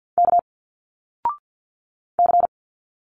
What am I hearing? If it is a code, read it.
Morse code, heard as a keyed tone, about 50 wpm, 700 Hz. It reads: K C
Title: Single Letters 50wpm